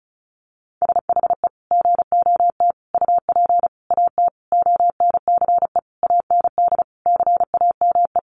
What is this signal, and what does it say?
Morse code, heard as a keyed tone, about 34 wpm, 700 Hz. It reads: SHE GOT UP AT ONCE AND CAME
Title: Sentences from Top 1100 Words 35wpm